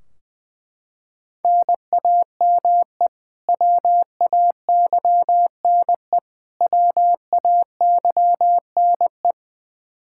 Morse code, heard as a keyed tone, about 20 wpm, 700 Hz. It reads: NAME WAYNE WAYNE
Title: Practice QSO - Exchange 1 - Beginning - 20wpm